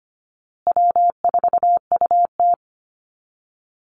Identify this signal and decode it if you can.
Morse code, heard as a keyed tone, about 25 wpm, 700 Hz. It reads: W4UT
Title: Easy Call Signs Repeated 3x - 25wpm